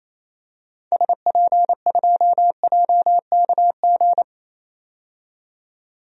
Morse code, heard as a keyed tone, about 28 wpm, 700 Hz. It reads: SP2JKG
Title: DXCC-100 Call Signs - 6 Characters - 28wpm